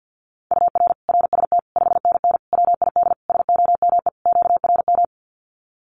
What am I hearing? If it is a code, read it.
Morse code, heard as a keyed tone, about 50 wpm, 700 Hz. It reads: UR RST 5NN WID SOME QRM
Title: Common QSO Phrases 50wpm